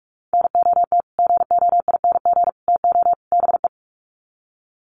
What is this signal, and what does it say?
Morse code, heard as a keyed tone, about 44 wpm, 700 Hz. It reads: NOT GOING TO BE
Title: Sets of 4 Words 45wpm